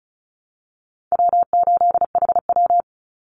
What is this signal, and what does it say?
Morse code, heard as a keyed tone, about 34 wpm, 700 Hz. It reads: W8HW
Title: Easy Call Signs Repeated 2x - 35wpm